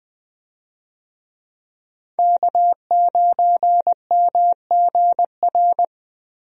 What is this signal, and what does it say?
Morse code, heard as a keyed tone, about 20 wpm, 700 Hz. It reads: K9MGR
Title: USA Call Signs - 5 Characters - Repeated 3x - 20wpm